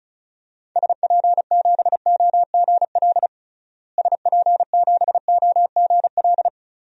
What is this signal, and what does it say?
Morse code, heard as a keyed tone, about 36 wpm, 700 Hz. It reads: SP7OGL SP7OGL
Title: DXCC-100 Call Signs - 6 Characters - Repeated 2x - 35wpm